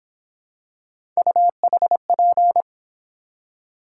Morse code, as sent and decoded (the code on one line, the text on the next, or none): ..- .... .--.
UHP